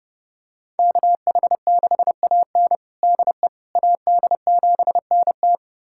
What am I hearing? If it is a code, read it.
Morse code, heard as a keyed tone, about 30 wpm, 700 Hz. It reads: KH6AN DE AD7NT
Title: USA Callsign DE Callsign 30wpm